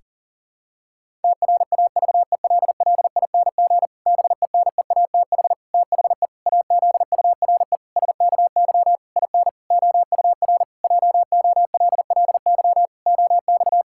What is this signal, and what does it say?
Morse code, heard as a keyed tone, about 40 wpm, 700 Hz. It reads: TRAVELLING BENEATH THE AZURE SKY IN OUR JOLLY OX
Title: Pangram Challenge v2 - 40wpm